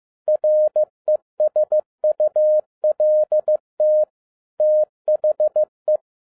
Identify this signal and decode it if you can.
Morse code, heard as a keyed tone, about 15 wpm, 610 Hz. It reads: RESULT THE
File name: FIELD DAY CONTEST QRV FROM SAKAE VILLAGE 15wpm